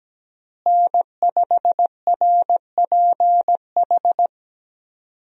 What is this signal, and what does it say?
Morse code, heard as a keyed tone, about 17 wpm, 700 Hz. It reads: N5RPH
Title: USA Call Signs - 5 Characters - Repeated 2x - 17wpm